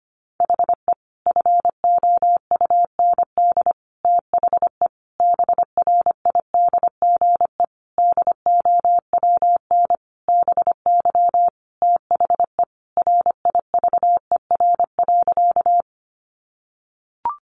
Morse code, heard as a keyed tone, about 25 wpm, 700 Hz. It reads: HE FOUND THE BRIDGE DOWN BY THE RIVER.
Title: Sentences from Top 1200 Words 25wpm